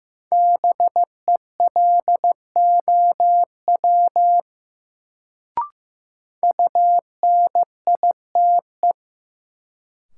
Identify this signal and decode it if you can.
Morse code, heard as a keyed tone, about 15 wpm, 700 Hz. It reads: BELOW UNITE
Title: Sentences from Top 1000 Words - Review Part 5 - New Word Drill - 15wpm